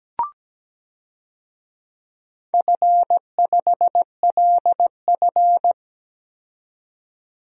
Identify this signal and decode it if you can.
Morse code, heard as a keyed tone, about 17 wpm, 700 Hz. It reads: F5LF
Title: DXCC-100 Call Signs - 4 Characters - 17wpm